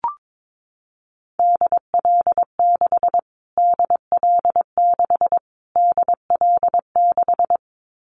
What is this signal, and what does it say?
Morse code, heard as a keyed tone, about 22 wpm, 700 Hz. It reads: DL6 DL6 DL6
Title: Callsign Prefixes for Top 25 DXCC - Repeated 3x - 22wpm